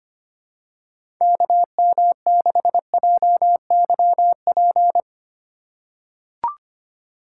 Morse code, heard as a keyed tone, about 25 wpm, 700 Hz. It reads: KM6JYP
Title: USA Call Signs - 6 Characters - 2x Character Spacing - 25wpm